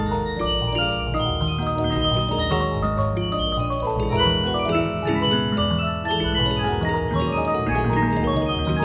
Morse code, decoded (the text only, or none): none